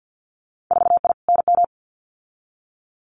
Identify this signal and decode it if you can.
Morse code, heard as a keyed tone, about 50 wpm, 700 Hz. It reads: 4I NM
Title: Winter Field Day Exchanges Encore 50wpm